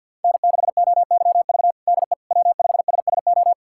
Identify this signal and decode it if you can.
Morse code, heard as a keyed tone, about 48 wpm, 700 Hz. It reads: N6OYV DE W5SSO
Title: USA Callsign DE Callsign - 5 Characters - 50wpm